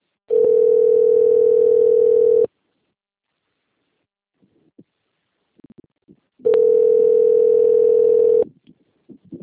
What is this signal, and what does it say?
Ringback tone (North America), 440 + 480 Hz.
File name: rg-1600-unknown- - - .52191